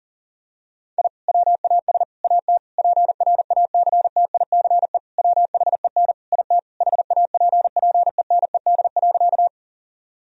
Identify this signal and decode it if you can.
Morse code, heard as a keyed tone, about 37 wpm, 700 Hz. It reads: I WAS AT PRACTICE WHEN IT HAPPENED.
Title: Sentences from Top 1000 Words - Review Part 4 - 40wpm